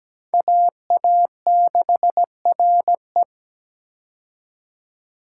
Morse code, heard as a keyed tone, about 17 wpm, 700 Hz. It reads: AA6RE
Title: K1USN SST QSO Practice - 17wpm